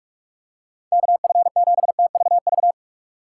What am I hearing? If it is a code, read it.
Morse code, heard as a keyed tone, about 43 wpm, 700 Hz. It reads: KF7TVV
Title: USA Call Signs - 6 Characters - 6x Character Spacing - 45wpm